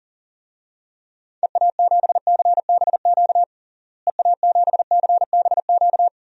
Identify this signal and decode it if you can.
Morse code, heard as a keyed tone, about 40 wpm, 700 Hz. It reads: EA7CBQ EA7CBQ
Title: DXCC-100 Call Signs - 6 Characters - Repeated 3x - 40wpm